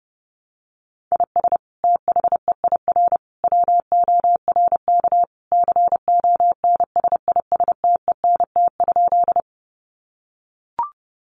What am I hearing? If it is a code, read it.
Morse code, heard as a keyed tone, about 30 wpm, 700 Hz. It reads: IS THEIR WORK CONSISTENT?
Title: Sentences from Top 1500 Words 30wpm